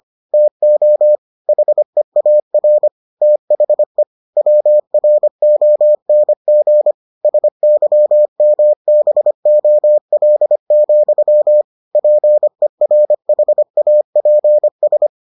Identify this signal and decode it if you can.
Morse code, heard as a keyed tone, about 25 wpm, 600 Hz. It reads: TO HEAR THE WRONG SYMBOL, PERHAPS